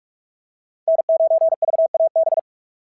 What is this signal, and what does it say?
Morse code, heard as a keyed tone, about 48 wpm, 690 Hz. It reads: N9VAB
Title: SOTA QSO Practice 45wpm